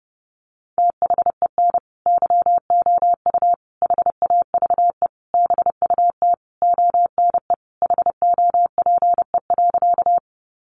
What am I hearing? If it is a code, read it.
Morse code, heard as a keyed tone, about 30 wpm, 700 Hz. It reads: THEN YOU HAVE BUT ONE HOPE.
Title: Sentences from Top 1000 Words 30wpm